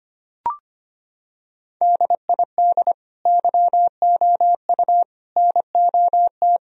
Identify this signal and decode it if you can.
Morse code, heard as a keyed tone, about 25 wpm, 700 Hz. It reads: DID YOU NOT
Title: Sentences from Top 1000 Words - Review Part 4 - 25wpm